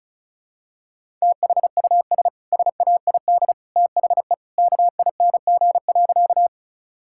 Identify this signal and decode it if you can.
Morse code, heard as a keyed tone, about 36 wpm, 700 Hz. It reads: THUS SAID THE KING.
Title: Sentences from Top 1000 Words - Review Part 5 - 35wpm